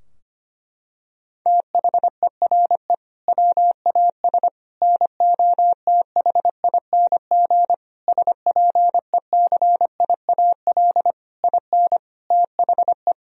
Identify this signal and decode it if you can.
Morse code, heard as a keyed tone, about 25 wpm, 700 Hz. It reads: THERE WAS NOTHING SPECIAL IN THE
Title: Sentences from Top 1000 Words - Review Part 3 - 25wpm